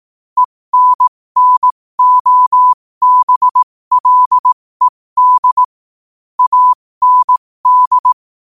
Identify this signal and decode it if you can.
Morse code, heard as a keyed tone, about 18 wpm, spaced as for 15 wpm, 1000 Hz. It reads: ENNOBLED AND
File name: qotdfiles / 15wpm